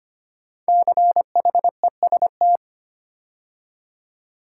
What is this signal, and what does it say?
Morse code, heard as a keyed tone, about 25 wpm, 700 Hz. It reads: CHEST
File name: 5 Letter Words - First 3 Letters Given - 25wpm